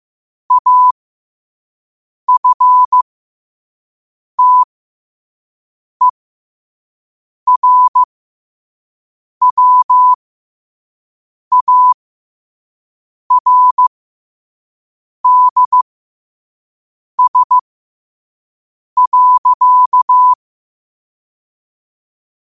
Morse code, heard as a keyed tone, about 15 wpm, spaced as for 5 wpm, 1000 Hz. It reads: AFTERWARDS.
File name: qotdfiles / 5wpm